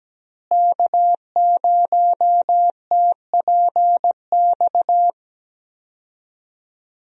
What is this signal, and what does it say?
Morse code, heard as a keyed tone, about 17 wpm, 700 Hz. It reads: K0TPX
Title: USA Call Signs - 5 Characters - Repeated 2x - 17wpm